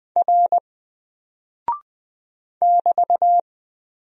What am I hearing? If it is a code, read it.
Morse code, heard as a keyed tone, about 20 wpm, 700 Hz. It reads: R <BT>
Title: Single Character (Letter:Number:Punctuation:Prosign) Alt 20wpm